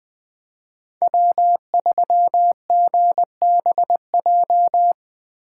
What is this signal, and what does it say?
Morse code, heard as a keyed tone, about 20 wpm, 700 Hz. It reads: W3GBJ
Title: Speed Racing - USA 1x3 Call Signs 17wpm